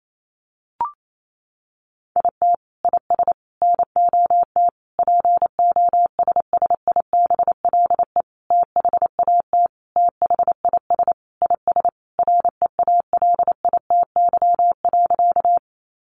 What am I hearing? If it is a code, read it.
Morse code, heard as a keyed tone, about 28 wpm, 700 Hz. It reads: IT IS NOT POSSIBLE THAT THIS IS REALITY.
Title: Sentences from Top 1200 Words 28wpm